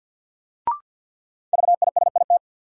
Morse code, heard as a keyed tone, about 52 wpm, 700 Hz. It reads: VISIT
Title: Sentences from Top 900 Words - New Word Drill - 50wpm